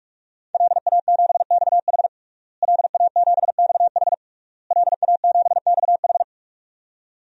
Morse code, heard as a keyed tone, about 46 wpm, 700 Hz. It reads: LA7XH LA7XH LA7XH
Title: DXCC-100 Call Signs - 5 Characters - Repeated 3x - 45wpm